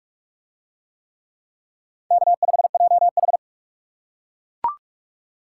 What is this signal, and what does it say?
Morse code, heard as a keyed tone, about 40 wpm, 700 Hz. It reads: K5JH